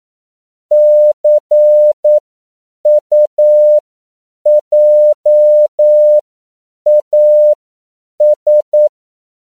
Morse code, heard as a keyed tone, about 9 wpm, spaced as for 7 wpm, 600 Hz. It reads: CUJAS
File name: CW / 10PPM